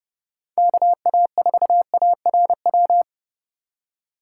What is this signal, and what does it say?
Morse code, heard as a keyed tone, about 37 wpm, 700 Hz. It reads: KA4ARW
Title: Speed Racing - USA 2x3 Call Signs 30wpm